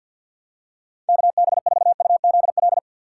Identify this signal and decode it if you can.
Morse code, heard as a keyed tone, about 48 wpm, 690 Hz. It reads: KB4UZL
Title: USA Call Signs - 3x Character Spacing - 50wpm